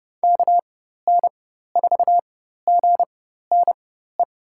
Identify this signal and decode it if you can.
Morse code, heard as a keyed tone, about 32 wpm, spaced as for 14 wpm, 700 Hz. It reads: KN4GNE
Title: USA Call Signs - 6 Characters - 4x Character Spacing - 30wpm